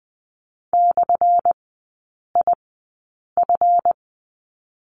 Morse code, heard as a keyed tone, about 20 wpm, 700 Hz. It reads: / I F
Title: 198 Lesson 32e - Letter k - Three Characters 20wpm